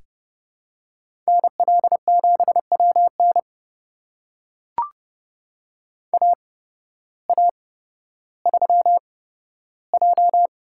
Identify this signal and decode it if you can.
Morse code, heard as a keyed tone, about 30 wpm, spaced as for 14 wpm, 700 Hz. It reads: NL7WN AA3J
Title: USA Call Signs - 5 Characters - 8x Character Spacing - 30wpm